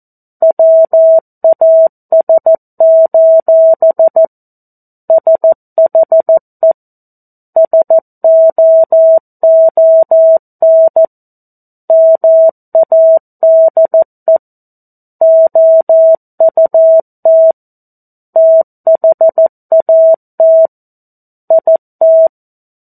Morse code, heard as a keyed tone, about 14 wpm, 650 Hz. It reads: WAS: SHE SOON MADE OUT THAT IT
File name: 02-Chapter-02 14 WPM